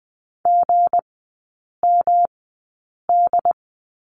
Morse code, heard as a keyed tone, about 20 wpm, spaced as for 8 wpm, 700 Hz. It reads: GMD
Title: 198 Lesson 32e - Letter k - Three Characters 20wpm